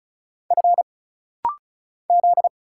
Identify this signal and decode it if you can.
Morse code, heard as a keyed tone, about 36 wpm, 700 Hz. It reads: F Z
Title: Single Letters - Rapid-Fire 35wpm